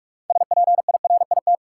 Morse code, heard as a keyed tone, about 46 wpm, 700 Hz. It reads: SPIRIT
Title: Sentences from Top 1400 Words - New Word Drill Alt - 45wpm